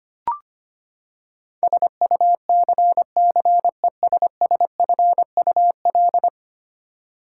Morse code, heard as a keyed tone, about 25 wpm, 700 Hz. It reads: SUCCESSFUL
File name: Ten Letter Words - 25wpm